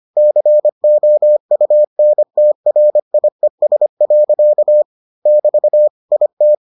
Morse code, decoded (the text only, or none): COUNTRIES. <BT> IT